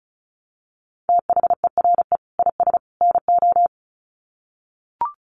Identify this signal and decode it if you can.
Morse code, heard as a keyed tone, about 34 wpm, 700 Hz. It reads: THERE IS NO
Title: Sets of 3 Words 35wpm